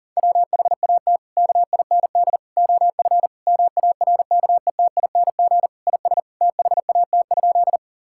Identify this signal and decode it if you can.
Morse code, heard as a keyed tone, about 40 wpm, 700 Hz. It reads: WHAT KIND OF MARKETING IS THAT?
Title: Sentences from Top 1000 Words - Review Part 5 - 40wpm